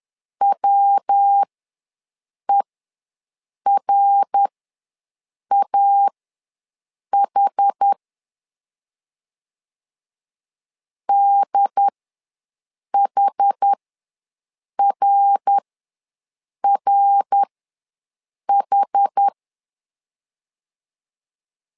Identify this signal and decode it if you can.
Morse code, heard as a keyed tone, about 11 wpm, spaced as for 6 wpm, 800 Hz. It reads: WERAH DHRRH
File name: LEZIONI / 3 MWRO / 3 MWR ws 10wpm 2151